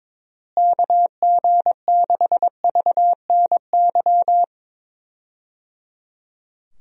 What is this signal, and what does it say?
Morse code, heard as a keyed tone, about 22 wpm, 700 Hz. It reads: KG6VNY
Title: USA Call Signs - 6 Characters - Repeated 3x - 22wpm